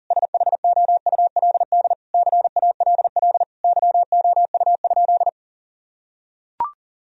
Morse code, heard as a keyed tone, about 40 wpm, 700 Hz. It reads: SHOULD CALL YOU?
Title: Sentences from Top 1000 Words - Review Part 3 - 40wpm